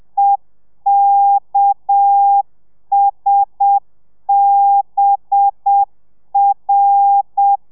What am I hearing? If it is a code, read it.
Morse code, heard as a keyed tone, about 7 wpm, 800 Hz. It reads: EKSBR